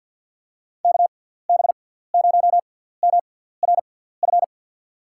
Morse code, heard as a keyed tone, about 47 wpm, 700 Hz. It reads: K B 0 M R F
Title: USA Call Signs - 6 Characters - 6x Character Spacing - 50wpm